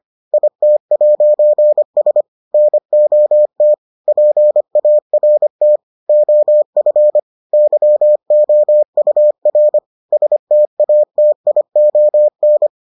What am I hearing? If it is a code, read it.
Morse code, heard as a keyed tone, about 25 wpm, 600 Hz. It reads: IT'S NOT PART OF YOUR STATION